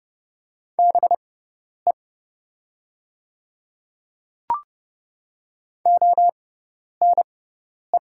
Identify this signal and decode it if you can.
Morse code, heard as a keyed tone, about 31 wpm, spaced as for 10 wpm, 700 Hz. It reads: BE ONE
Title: 6x Character Spacing - Top 100 Words - 30wpm